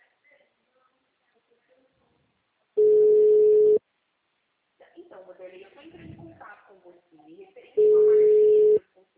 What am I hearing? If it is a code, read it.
Ringback tone (Europe, 425 Hz), with speech.